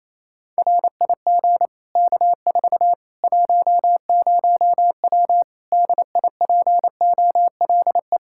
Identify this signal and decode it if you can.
Morse code, heard as a keyed tone, about 28 wpm, 700 Hz. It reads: RIG K4 10W DIPOLE
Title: Practice QSO - Exchange 2 - Info - 28wpm